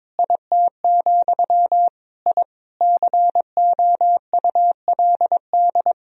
Morse code, heard as a keyed tone, about 22 wpm, 700 Hz. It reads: IT, I COULD
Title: Sentences from Top 1000 Words - Review Part 3 - 22wpm